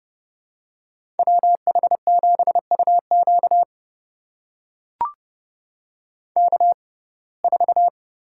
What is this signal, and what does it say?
Morse code, heard as a keyed tone, about 30 wpm, spaced as for 22 wpm, 700 Hz. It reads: WH7UQ K4
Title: USA Call Signs - 6x Character Spacing - 30wpm